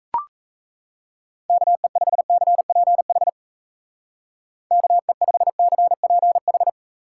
Morse code, heard as a keyed tone, about 35 wpm, 700 Hz. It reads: KE5CPH KE5CPH
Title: Speed Racing - USA 2x3 Call Signs 28wpm